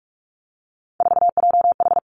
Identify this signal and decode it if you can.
Morse code, heard as a keyed tone, about 46 wpm, 700 Hz. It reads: 4JH
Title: Three Challenging Letter-Number Combinations 45wpm